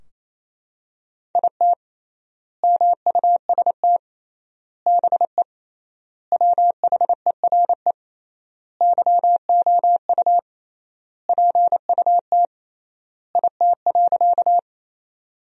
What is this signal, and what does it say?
Morse code, heard as a keyed tone, about 28 wpm, 700 Hz. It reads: IT MUST BE WHERE YOU PUT IT.
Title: 3x Word-Spacing - Sentences from Top 200 Words - 28wpm